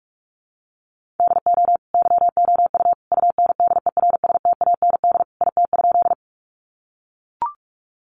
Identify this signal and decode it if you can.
Morse code, heard as a keyed tone, about 44 wpm, 700 Hz. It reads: DO YOU UNDERSTAND IT?
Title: Sentences from Top 600 Words 45wpm